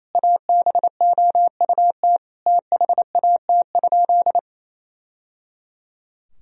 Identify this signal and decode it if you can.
Morse code, heard as a keyed tone, about 27 wpm, 700 Hz. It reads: ABOUT THAT?
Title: Sentences from Top 1000 Words - Review Part 4 - 28wpm